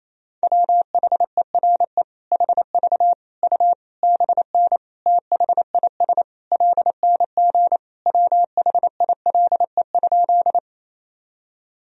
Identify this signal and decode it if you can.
Morse code, heard as a keyed tone, about 27 wpm, 700 Hz. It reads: WHERE HV U BN THIS LNG WHILE?
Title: Sentences from Top 300 Words and Abbreviations 28wpm